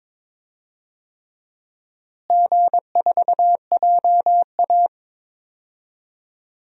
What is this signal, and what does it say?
Morse code, heard as a keyed tone, about 22 wpm, 700 Hz. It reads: G4JA
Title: DXCC-100 Call Signs - 4 Characters - Repeated 3x - 22wpm